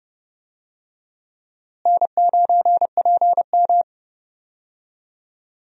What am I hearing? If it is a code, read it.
Morse code, heard as a keyed tone, about 30 wpm, 700 Hz. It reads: N9PM
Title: USA Call Signs - 4 Characters - Repeated 2x - 30wpm